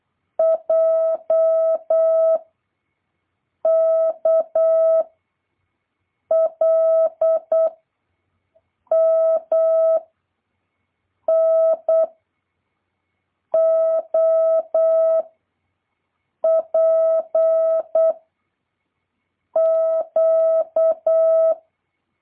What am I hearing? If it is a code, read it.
Morse code, heard as a keyed tone, about 8 wpm, 650 Hz. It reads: J K L M N O P Q
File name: jh8cbh / 2023cw